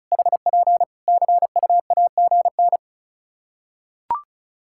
Morse code, heard as a keyed tone, about 36 wpm, 700 Hz. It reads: HP CUAGN
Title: Practice QSO - Exchange 4 - Ending - 35wpm